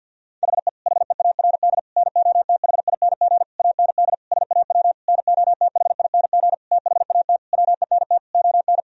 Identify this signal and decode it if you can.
Morse code, heard as a keyed tone, about 51 wpm, 700 Hz. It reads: HE HEARD NOTHING AND SAW NOTHING THAT WENT ON
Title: Sentences from Top 1000 Words - Review Part 3 - 50wpm